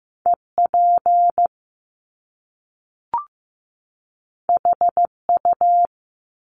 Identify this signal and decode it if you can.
Morse code, heard as a keyed tone, about 15 wpm, 700 Hz. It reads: EP HU
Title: Top Two Letter Combinations in Words 15wpm